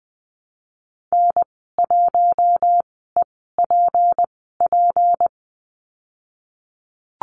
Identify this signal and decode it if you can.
Morse code, heard as a keyed tone, about 20 wpm, spaced as for 15 wpm, 700 Hz. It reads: N1EPP
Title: 154 Lesson 25g - Letter p - Call Signs 20wpm